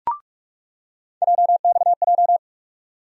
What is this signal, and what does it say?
Morse code, heard as a keyed tone, about 42 wpm, 700 Hz. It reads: JXJ
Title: USA Call Sign Postfixes - 3 Letters 45wpm